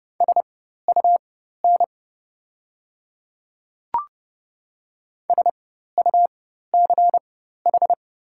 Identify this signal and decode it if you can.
Morse code, heard as a keyed tone, about 30 wpm, spaced as for 14 wpm, 690 Hz. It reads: SUN SUCH
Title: 4x Character Spacing - Top 500 Words - 30wpm